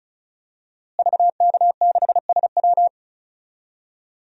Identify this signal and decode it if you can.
Morse code, heard as a keyed tone, about 34 wpm, 700 Hz. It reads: VK6SW
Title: DXCC-100 Call Signs - 5 Characters - Repeated 2x - 35wpm